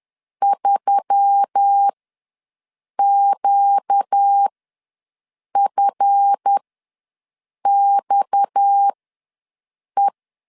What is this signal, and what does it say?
Morse code, heard as a keyed tone, about 10 wpm, spaced as for 6 wpm, 800 Hz. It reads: 3QFXE
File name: LEZIONI / 7 GF9Z / 7 GF ws 10wpm 3918